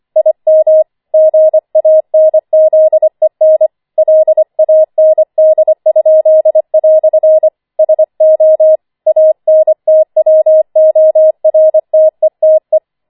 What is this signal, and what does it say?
Morse code, heard as a keyed tone, about 25 wpm, 630 Hz. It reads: IM GANZEN LAND?" SO ANTWORTETE
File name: 25wpm